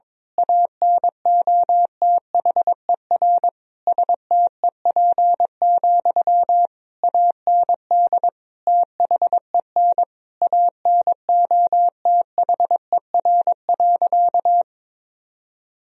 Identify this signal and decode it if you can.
Morse code, heard as a keyed tone, about 22 wpm, 700 Hz. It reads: ANOTHER STEP, AND THEN ANOTHER.
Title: Sentences from Top 1000 Words - Review Part 2 - 22wpm